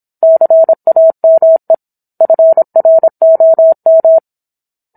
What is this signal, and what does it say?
Morse code, heard as a keyed tone, about 27 wpm, 650 Hz. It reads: CAME FROM